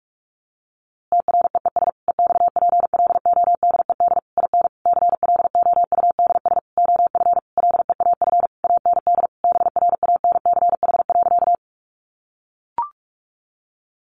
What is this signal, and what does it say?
Morse code, heard as a keyed tone, about 44 wpm, 700 Hz. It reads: TREES EXPLODED IN CLOUDS OF LEAF AND BRANCH.
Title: A Sound Of Thunder 45wpm